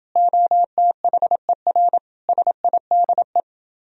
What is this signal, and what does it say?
Morse code, heard as a keyed tone, about 29 wpm, 700 Hz. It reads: OTHER SIDE